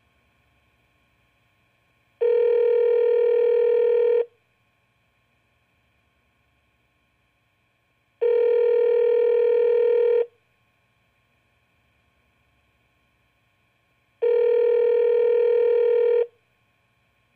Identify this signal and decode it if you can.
Ringback tone (North America), 440 + 480 Hz.